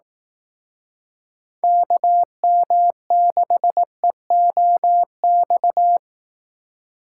Morse code, heard as a keyed tone, about 18 wpm, 700 Hz. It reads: KM6EOX
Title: Speed Racing - USA 2x3 Call Signs 15wpm